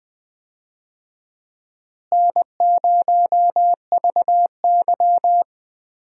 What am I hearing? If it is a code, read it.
Morse code, heard as a keyed tone, about 20 wpm, 700 Hz. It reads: N0VY
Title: USA Call Signs - 4 Characters - Repeated 2x - 20wpm